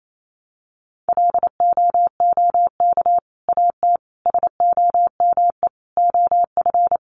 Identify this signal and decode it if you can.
Morse code, heard as a keyed tone, about 28 wpm, 700 Hz. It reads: LOOK AT SOME OF
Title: Sets of 5 Words Encore 28wpm